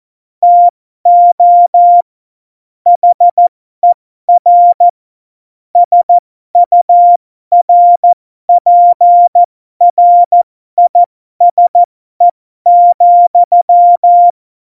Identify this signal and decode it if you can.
Morse code, heard as a keyed tone, about 14 wpm, spaced as for 12 wpm, 700 Hz. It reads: TO HER SURPRISE,
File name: Morsekurs / Witze